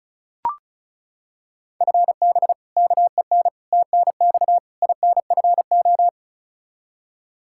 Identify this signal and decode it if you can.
Morse code, heard as a keyed tone, about 36 wpm, 700 Hz. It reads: FB KEN TNX INFO
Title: Practice QSO - Exchange 3 - More Info - 35wpm